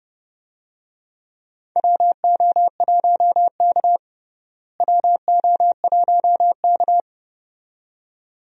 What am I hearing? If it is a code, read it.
Morse code, heard as a keyed tone, about 30 wpm, 700 Hz. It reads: WO1K WO1K
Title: USA Call Signs - 4 Characters - Repeated 2x - 30wpm